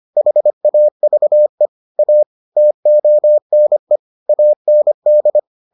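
Morse code, heard as a keyed tone, about 25 wpm, 600 Hz. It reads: HAVE A TONE AND
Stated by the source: Morse edition of the Foundations of Amateur Radio episodes.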